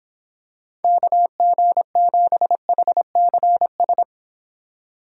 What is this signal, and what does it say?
Morse code, heard as a keyed tone, about 25 wpm, 700 Hz. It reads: KG7HCS